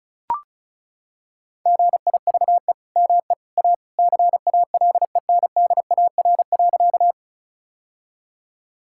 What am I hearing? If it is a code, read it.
Morse code, heard as a keyed tone, about 34 wpm, 700 Hz. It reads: GIVE ME A CALENDAR.
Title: Sentences from Top 1600 Words 35wpm